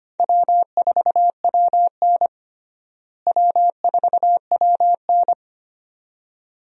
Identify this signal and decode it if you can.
Morse code, heard as a keyed tone, about 25 wpm, 700 Hz. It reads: W4WN W4WN
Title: USA Call Signs - 4 Characters - Repeated 2x - 25wpm